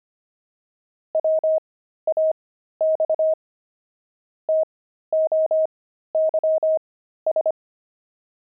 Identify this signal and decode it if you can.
Morse code, heard as a keyed tone, about 25 wpm, spaced as for 13 wpm, 630 Hz. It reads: WAX TOYS
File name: pangram-fs-13-cs-25